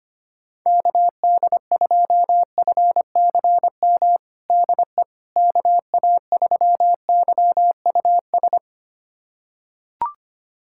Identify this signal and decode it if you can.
Morse code, heard as a keyed tone, about 25 wpm, 700 Hz. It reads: KD2FCM DE KA3YUS
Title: USA Callsign DE Callsign - 6 Characters - 25wpm